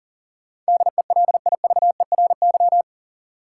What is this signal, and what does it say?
Morse code, heard as a keyed tone, about 40 wpm, 700 Hz. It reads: DELIVERY
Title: Sentences from Top 1500 Words - New Word Drill Alt - 40wpm